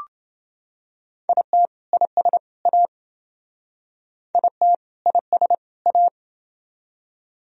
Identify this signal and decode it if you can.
Morse code, heard as a keyed tone, about 28 wpm, 700 Hz. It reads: IT IS A IT IS A